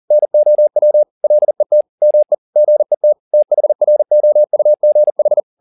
Morse code, heard as a keyed tone, about 40 wpm, 600 Hz. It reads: NOW LET ME GET THROUGH
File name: the old man and the sea 2 40wpm